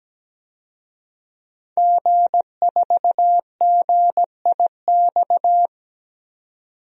Morse code, heard as a keyed tone, about 17 wpm, 700 Hz. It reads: G4GIX
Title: DXCC-100 Call Signs - 5 Characters - 17wpm